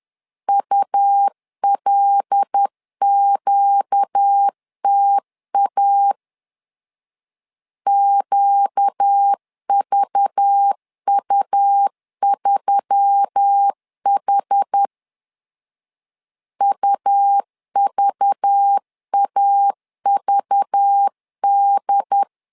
Morse code, heard as a keyed tone, about 10 wpm, 800 Hz. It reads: ULQTA QVU3H UVAVD
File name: LEZIONI / 5 LBVC / 5 LBV 10wpm 2248